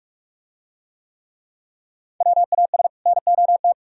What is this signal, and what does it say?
Morse code, heard as a keyed tone, about 44 wpm, 700 Hz. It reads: WAS NOT